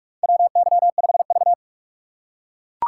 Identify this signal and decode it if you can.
Morse code, heard as a keyed tone, about 42 wpm, 700 Hz. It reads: WY5V
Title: USA Call Signs - 4 Characters - 2x Character Spacing - 45wpm